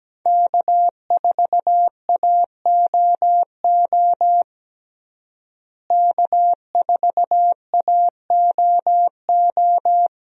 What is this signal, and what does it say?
Morse code, heard as a keyed tone, about 17 wpm, 700 Hz. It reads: K4AOO K4AOO
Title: USA Call Signs - 5 Characters - Repeated 3x - 17wpm